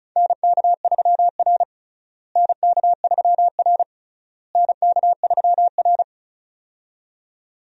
Morse code, heard as a keyed tone, about 36 wpm, 700 Hz. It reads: NK3R NK3R NK3R
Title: USA Call Signs - 4 Characters - Repeated 3x - 35wpm